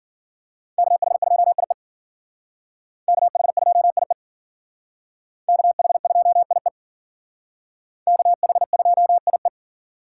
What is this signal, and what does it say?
Morse code, heard as a keyed tone, about 52 wpm, 700 Hz. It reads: KH2IE KH2IE KH2IE KH2IE